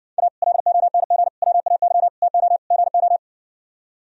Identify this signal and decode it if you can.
Morse code, heard as a keyed tone, about 60 wpm, 700 Hz. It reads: A LONG WAY TO GO